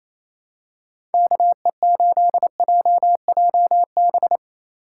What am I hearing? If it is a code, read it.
Morse code, heard as a keyed tone, about 27 wpm, 700 Hz. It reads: KE8JJB
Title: USA Call Signs - 6 Characters - 3x Character Spacing - 28wpm